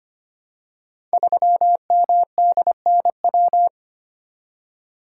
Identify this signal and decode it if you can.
Morse code, heard as a keyed tone, about 25 wpm, 700 Hz. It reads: 3MDNW
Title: Code Groups - 5 Letter:Numbers - 25wpm